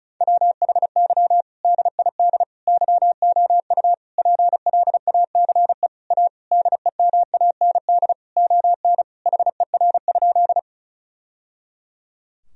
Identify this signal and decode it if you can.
Morse code, heard as a keyed tone, about 36 wpm, 700 Hz. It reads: WHY DID YOU PLACE A DEMAND ON HER?
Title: Sentences from Top 1000 Words - Review Part 4 - 35wpm